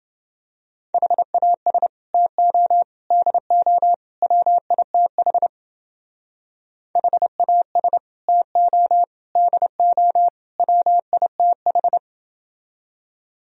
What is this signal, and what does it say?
Morse code, heard as a keyed tone, about 28 wpm, 700 Hz. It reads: HAS TO DO WITH HAS TO DO WITH